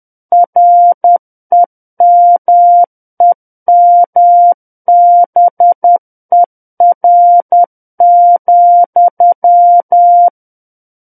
Morse code, heard as a keyed tone, about 10 wpm, 700 Hz. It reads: REMEMBER,
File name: Book One / 01-Chapter 1 - 10 WPM